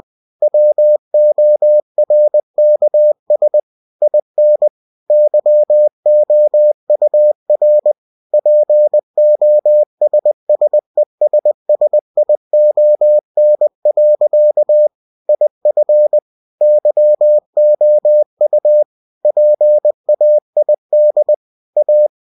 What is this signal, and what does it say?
Morse code, heard as a keyed tone, about 20 wpm, 600 Hz. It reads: WORKS IN YOUR POSSESSION. IF YOU PAID A
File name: cw-books / winnie-the-pooh / winnie-20wpm-page-0113